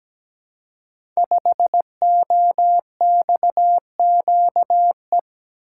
Morse code, heard as a keyed tone, about 17 wpm, 700 Hz. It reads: 5OXQE
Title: Code Groups - 5 Letter:Numbers - 17wpm